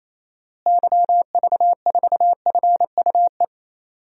Morse code, heard as a keyed tone, about 28 wpm, 700 Hz. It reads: YV4FUE
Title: DXCC-100 Call Signs - 6 Characters - Repeated 2x - 28wpm